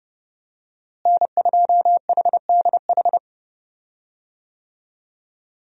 Morse code, heard as a keyed tone, about 30 wpm, 700 Hz. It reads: N2HDH
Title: USA Call Signs - 5 Characters - Repeated 2x - 30wpm